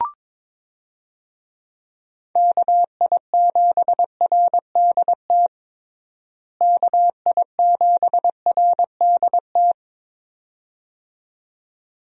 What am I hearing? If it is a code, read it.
Morse code, heard as a keyed tone, about 22 wpm, 700 Hz. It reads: KI7RDT KI7RDT
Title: USA Call Signs - 6 Characters - Repeated 2x - 22wpm